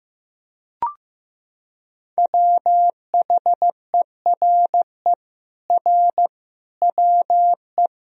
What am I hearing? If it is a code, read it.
Morse code, heard as a keyed tone, about 16 wpm, 700 Hz. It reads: WHERE R WE
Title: Sentences from Top 200 Words and Abbreviations 15wpm